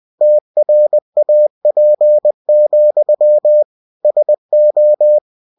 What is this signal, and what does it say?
Morse code, heard as a keyed tone, about 20 wpm, 600 Hz. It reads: TRAP, SO
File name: cw-books / winnie-the-pooh / winnie-20wpm-page-0034